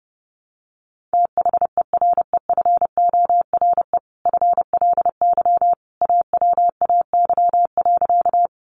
Morse code, heard as a keyed tone, about 28 wpm, 700 Hz. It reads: THEREFORE FLY AWAY.
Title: Sentences from Top 800 Words 30wpm